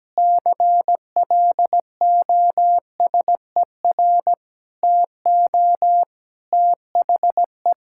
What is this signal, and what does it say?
Morse code, heard as a keyed tone, about 17 wpm, 700 Hz. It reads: CLOSER TO THE
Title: Pangram Challenge v2 - 17wpm